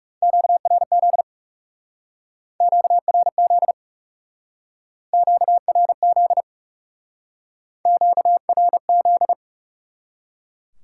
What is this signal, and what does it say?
Morse code, heard as a keyed tone, about 38 wpm, 700 Hz. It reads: QRZ QRZ QRZ QRZ